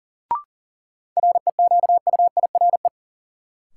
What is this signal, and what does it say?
Morse code, heard as a keyed tone, about 40 wpm, 700 Hz. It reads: REQUIRE
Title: Sentences from Top 900 Words - New Word Drill - 40wpm